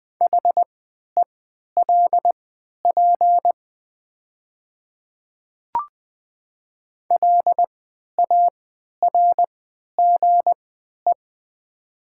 Morse code, heard as a keyed tone, about 20 wpm, spaced as for 11 wpm, 700 Hz. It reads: HELP LARGE
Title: 3x Character Spacing - Top 500 Words - 20wpm